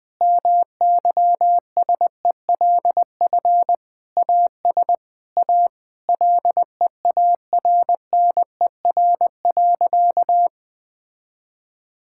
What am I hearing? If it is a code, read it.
Morse code, heard as a keyed tone, about 21 wpm, 700 Hz. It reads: MYSELF AS A LEARNER.
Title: Sentences with 8-Word Phrases - 20wpm